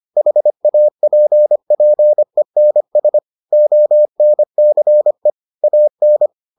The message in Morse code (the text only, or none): HAPPENS ONCE AN